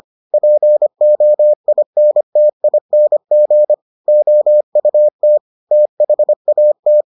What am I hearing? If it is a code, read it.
Morse code, heard as a keyed tone, about 25 wpm, 600 Hz. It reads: POINTING OUT THAT